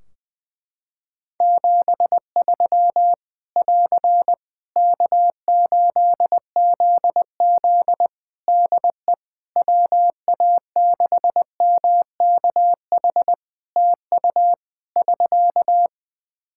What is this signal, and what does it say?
Morse code, heard as a keyed tone, about 21 wpm, 700 Hz. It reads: 73 <AR> K8ZZ DE WA6MKH TU <SK>
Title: K1USN SST QSO Practice 20wpm